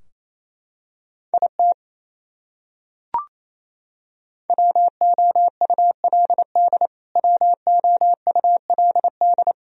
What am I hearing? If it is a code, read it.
Morse code, heard as a keyed tone, about 28 wpm, 700 Hz. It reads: IT WOULD WOULD
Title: Top 100 Words Repeated 28wpm